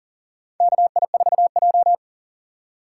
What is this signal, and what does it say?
Morse code, heard as a keyed tone, about 40 wpm, 700 Hz. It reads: KI4J
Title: USA Call Signs - 4 Characters - 4x Character Spacing - 40wpm